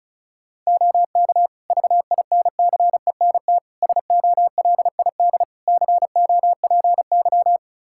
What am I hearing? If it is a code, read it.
Morse code, heard as a keyed tone, about 36 wpm, 700 Hz. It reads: OK VINCENT SOLID COPY
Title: Practice QSO - Exchange 3 - More Info - 35wpm